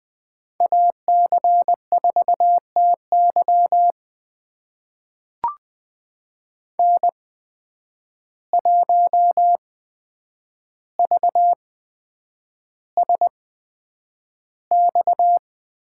Morse code, heard as a keyed tone, about 20 wpm, spaced as for 9 wpm, 700 Hz. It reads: AC4TY N1VSX
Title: USA Call Signs - 5 Characters - 8x Character Spacing - 20wpm